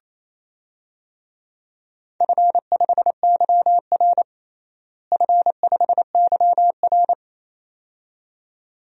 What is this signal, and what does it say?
Morse code, heard as a keyed tone, about 28 wpm, 700 Hz. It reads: F5YR F5YR
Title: DXCC-100 Call Signs - 4 Characters - Repeated 2x - 28wpm